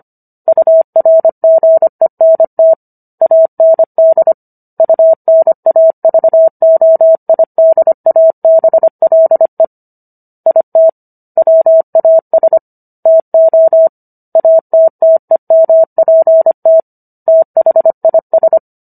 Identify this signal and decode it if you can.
Morse code, heard as a keyed tone, about 25 wpm, 650 Hz. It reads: URGENT AND UNAVOIDABLE IT WAS TO ATTEMPT THIS